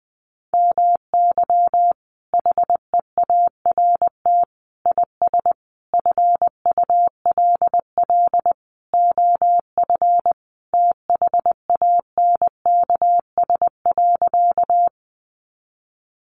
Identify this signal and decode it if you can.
Morse code, heard as a keyed tone, about 20 wpm, 700 Hz. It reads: MY HEART IS FULL OF THANKS.
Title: Sentences from Top 1200 Words 20wpm